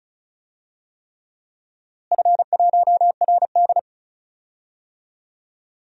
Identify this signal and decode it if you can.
Morse code, heard as a keyed tone, about 34 wpm, 700 Hz. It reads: F1RD
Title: DXCC-100 Call Signs - 4 Characters - Repeated 3x - 35wpm